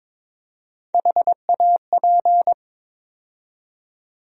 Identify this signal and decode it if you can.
Morse code, heard as a keyed tone, about 22 wpm, 700 Hz. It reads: HAP
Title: USA Call Sign Postfixes - 3 Letters 22wpm